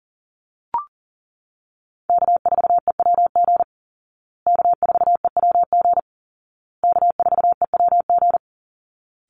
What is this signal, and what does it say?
Morse code, heard as a keyed tone, about 40 wpm, 700 Hz. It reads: K4EWG K4EWG K4EWG
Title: Call Signs Repeated 2x - 40wpm